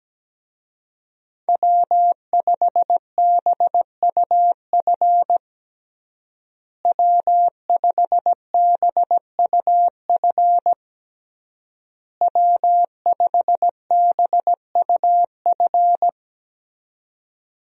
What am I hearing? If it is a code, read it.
Morse code, heard as a keyed tone, about 17 wpm, 700 Hz. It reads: W5BUF W5BUF W5BUF
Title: USA Call Signs - 5 Characters - Repeated 3x - 17wpm